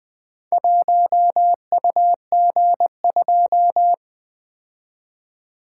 Morse code, heard as a keyed tone, about 24 wpm, 700 Hz. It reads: 1UG2
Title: Speed-Racing - Four Letter-Number Combinations - 20wpm